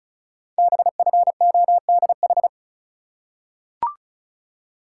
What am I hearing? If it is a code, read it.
Morse code, heard as a keyed tone, about 36 wpm, 690 Hz. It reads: BFODH
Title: Code Groups - 5 Letters - 35wpm